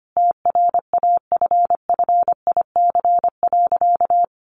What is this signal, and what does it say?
Morse code, heard as a keyed tone, about 25 wpm, 700 Hz. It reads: TRAFFIC.
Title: Sentences from Top 1300 Words 25wpm